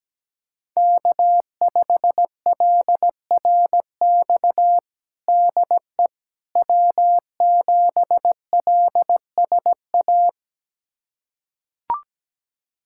Morse code, heard as a keyed tone, about 17 wpm, 700 Hz. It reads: K5LRX DE W7LSA
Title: USA Callsign DE Callsign 17wpm